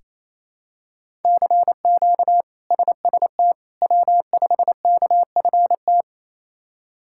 Morse code, heard as a keyed tone, about 27 wpm, 700 Hz. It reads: CQ SST W5KFT
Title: K1USN SST QSO Practice - 28wpm